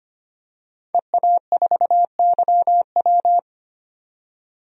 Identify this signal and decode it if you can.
Morse code, heard as a keyed tone, about 25 wpm, 700 Hz. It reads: EA4YW
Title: DXCC-100 Call Signs - 5 Characters - 25wpm